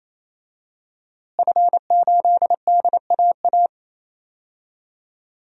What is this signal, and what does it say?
Morse code, heard as a keyed tone, about 28 wpm, 700 Hz. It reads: F8DAA
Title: DXCC-100 Call Signs - 5 Characters - Repeated 3x - 28wpm